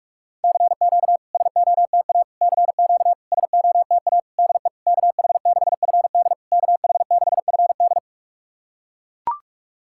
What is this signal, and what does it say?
Morse code, heard as a keyed tone, about 41 wpm, 690 Hz. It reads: CQ SOTA CQ SOTA DE KH6FD KH6FD
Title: SOTA QSO Practice 45wpm